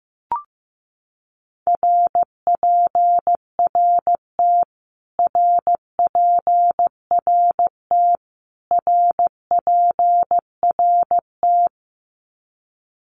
Morse code, heard as a keyed tone, about 15 wpm, 700 Hz. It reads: RPRT RPRT RPRT
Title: Abbreviations Repeated 3x - 15wpm